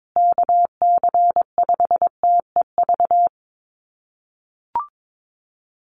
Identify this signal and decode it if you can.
Morse code, heard as a keyed tone, about 22 wpm, 700 Hz. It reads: KC5TEV
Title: USA Call Signs Repeated 2x 22wpm